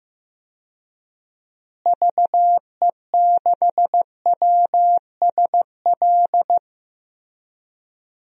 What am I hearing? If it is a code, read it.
Morse code, heard as a keyed tone, about 15 wpm, 700 Hz. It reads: VE6WSL
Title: DXCC-100 Call Signs - 6 Characters - Repeated 2x - 15wpm